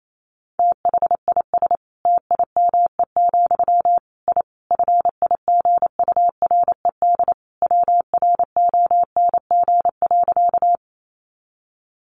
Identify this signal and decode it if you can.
Morse code, heard as a keyed tone, about 28 wpm, 700 Hz. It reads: THIS TIME, I FIGURED WRONG.
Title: A Sound Of Thunder 28wpm